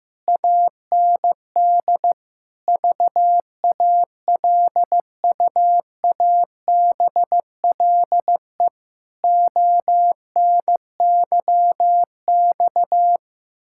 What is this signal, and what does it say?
Morse code, heard as a keyed tone, about 16 wpm, 700 Hz. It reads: AND VALUABLE ONYX
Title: Pangram Challenge v2 - 15wpm